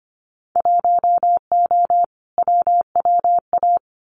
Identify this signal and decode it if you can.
Morse code, heard as a keyed tone, about 25 wpm, 700 Hz. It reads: 1O WWA
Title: Winter Field Day Exchanges 25wpm